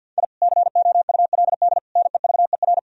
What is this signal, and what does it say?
Morse code, heard as a keyed tone, about 52 wpm, 700 Hz. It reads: I COULD NEVER
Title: Sentences from Top 1000 Words - Review Part 3 - 50wpm